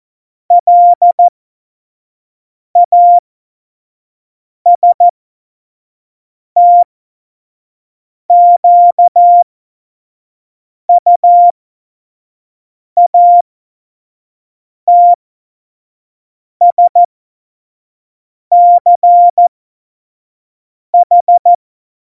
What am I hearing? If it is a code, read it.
Morse code, heard as a keyed tone, about 14 wpm, spaced as for 5 wpm, 700 Hz. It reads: LASTQUATSCH
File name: Morsekurs / Witze / Witz 19. Juni 2018, 25 F70